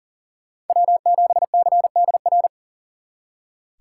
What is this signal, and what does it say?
Morse code, heard as a keyed tone, about 50 wpm, 700 Hz. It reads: W7CDR